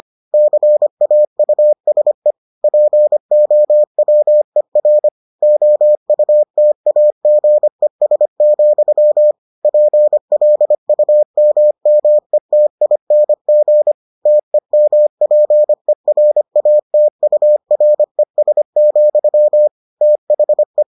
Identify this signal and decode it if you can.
Morse code, heard as a keyed tone, about 25 wpm, 600 Hz. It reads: CAUSE POWER OUTAGES, PLUMMETING TEMPERATURES, THE